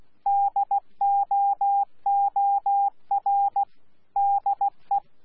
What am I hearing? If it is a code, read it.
Morse code, heard as a keyed tone, about 16 wpm, 800 Hz. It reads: DOOR DE
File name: Z PI4VRZ / PI4VRZ 14-jun-2025